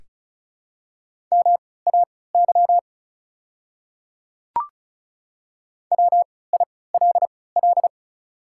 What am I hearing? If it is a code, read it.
Morse code, heard as a keyed tone, about 36 wpm, spaced as for 20 wpm, 700 Hz. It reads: MAY WILL
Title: 3x Character Spacing - Top 200 Words - 35wpm